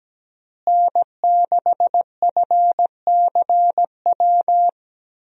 Morse code, heard as a keyed tone, about 17 wpm, 700 Hz. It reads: N6FCW
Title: USA Call Signs - 5 Characters - Repeated 3x - 17wpm